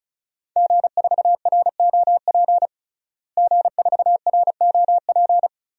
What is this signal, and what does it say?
Morse code, heard as a keyed tone, about 36 wpm, 700 Hz. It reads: G4ROP G4ROP
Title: DXCC-100 Call Signs - 5 Characters - Repeated 2x - 35wpm